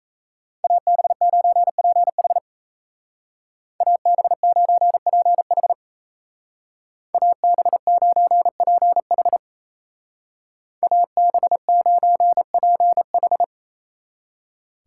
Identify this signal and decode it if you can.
Morse code, heard as a keyed tone, about 34 wpm, 700 Hz. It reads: AB9PH AB9PH AB9PH AB9PH
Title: Speed Racing - USA 2x2 Call Signs 28wpm